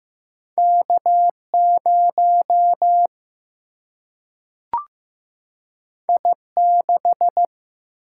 Morse code, heard as a keyed tone, about 15 wpm, 700 Hz. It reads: K 0 I 6
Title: Single-Letter Single-Number 15wpm